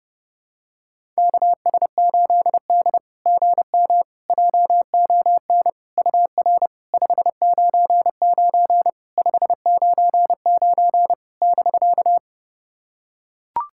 Morse code, heard as a keyed tone, about 29 wpm, 690 Hz. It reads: KS8D GM JON UR 599 599 <BK>
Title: SOTA QSO Practice 30wpm